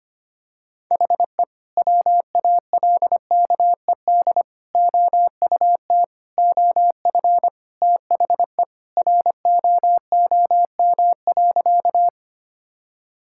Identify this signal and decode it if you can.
Morse code, heard as a keyed tone, about 25 wpm, 700 Hz. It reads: HE WALKED OUT OF THE ROOM.
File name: practice / 25wpm / Sentences with 2-Word Phrases - 25wpm